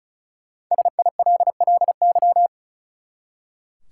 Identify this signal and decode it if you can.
Morse code, heard as a keyed tone, about 36 wpm, 700 Hz. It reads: SILLY
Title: Five Letter Words - 35wpm